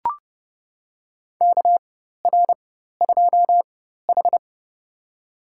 Morse code, heard as a keyed tone, about 29 wpm, spaced as for 14 wpm, 700 Hz. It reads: KR2H
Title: USA Call Signs - 4x Character Spacing - 30wpm